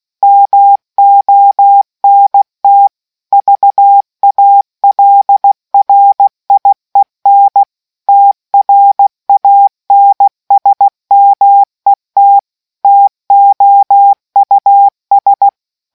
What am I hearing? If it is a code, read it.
Morse code, heard as a keyed tone, about 16 wpm, 800 Hz. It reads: MONT VALRIEN TRANSMET TOUS